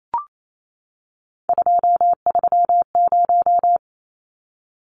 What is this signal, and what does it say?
Morse code, heard as a keyed tone, about 28 wpm, 700 Hz. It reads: 230
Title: Numbers - 3-Digit 28wpm